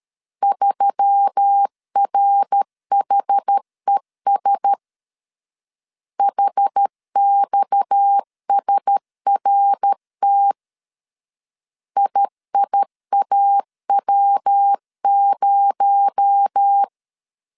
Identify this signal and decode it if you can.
Morse code, heard as a keyed tone, about 13 wpm, 800 Hz. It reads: 3RHES HXSRT IIAW0
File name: CW / DET / 1 6 1500C S A 12wpm 3440